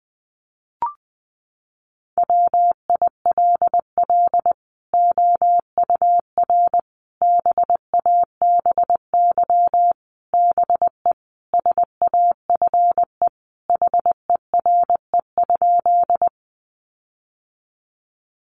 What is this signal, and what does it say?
Morse code, heard as a keyed tone, about 19 wpm, 700 Hz. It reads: WILL OUR BABY BE SAFE HERE?
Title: Sentences from Top 1300 Words 20wpm